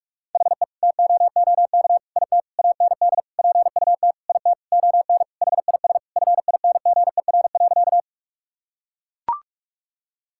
Morse code, heard as a keyed tone, about 41 wpm, 700 Hz. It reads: HE TOOK IT AND PUT IT ON HIS FINGER.
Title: Sentences from Top 1600 Words 45wpm